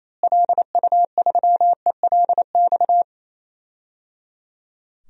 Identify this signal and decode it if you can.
Morse code, heard as a keyed tone, about 28 wpm, 700 Hz. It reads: LU3ELX
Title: DXCC-100 Call Signs - 6 Characters - Repeated 3x - 28wpm